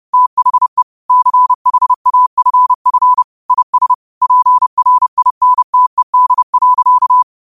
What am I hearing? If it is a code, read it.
Morse code, heard as a keyed tone, about 30 wpm, 1000 Hz. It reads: THE CHAFF IS PRINTED.
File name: qotdfiles / 30wpm